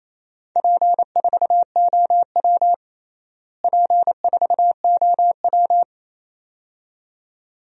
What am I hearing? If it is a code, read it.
Morse code, heard as a keyed tone, about 28 wpm, 700 Hz. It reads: P4OW P4OW
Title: DXCC-100 Call Signs - 4 Characters - Repeated 2x - 28wpm